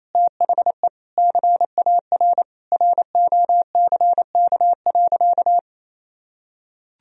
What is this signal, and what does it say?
Morse code, heard as a keyed tone, about 28 wpm, 700 Hz. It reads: THE CAR ROCK.
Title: Sentences from Top 1000 Words - Review Part 1 - 28wpm